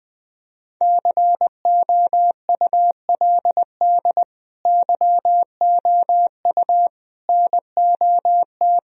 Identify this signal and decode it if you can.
Morse code, heard as a keyed tone, about 21 wpm, 700 Hz. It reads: COULD YOU NOT
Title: Sentences from Top 1000 Words - Review Part 1 - 20wpm